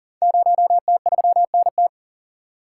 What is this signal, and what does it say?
Morse code, heard as a keyed tone, about 38 wpm, 690 Hz. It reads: 0T3NT
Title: Code Groups - 5 Letter:Numbers - 40wpm